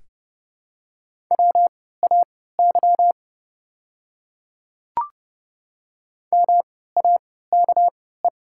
Morse code, heard as a keyed tone, about 31 wpm, spaced as for 17 wpm, 700 Hz. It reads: WAY MAKE
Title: 3x Character Spacing - Top 200 Words - 30wpm